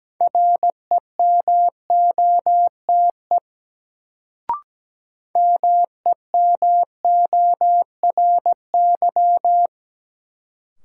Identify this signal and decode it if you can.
Morse code, heard as a keyed tone, about 17 wpm, 700 Hz. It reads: REMOTE MEMORY
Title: Sentences from Top 1300 Words - New Word Drill Alt - 17wpm